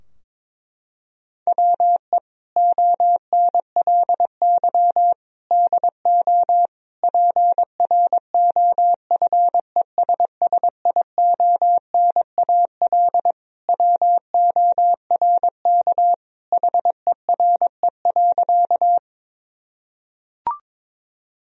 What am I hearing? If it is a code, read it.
Morse code, heard as a keyed tone, about 22 wpm, 700 Hz. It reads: WE ONLY DO PROFESSIONAL WORK HERE.
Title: Sentences from Top 1000 Words - Review Part 4 - 22wpm